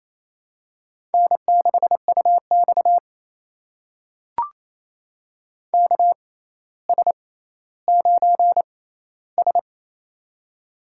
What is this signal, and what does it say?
Morse code, heard as a keyed tone, about 28 wpm, spaced as for 14 wpm, 700 Hz. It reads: N6UX KS9S
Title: USA Call Signs - 4 Characters - 6x Character Spacing - 28wpm